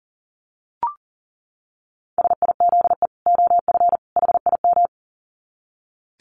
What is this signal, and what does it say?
Morse code, heard as a keyed tone, about 40 wpm, 700 Hz. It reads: SIZE OF HIM
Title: Dit Challenge 40wpm